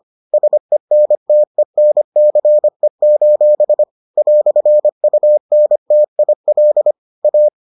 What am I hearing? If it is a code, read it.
Morse code, heard as a keyed tone, about 25 wpm, 600 Hz. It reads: SENTENCE: "UNTIL A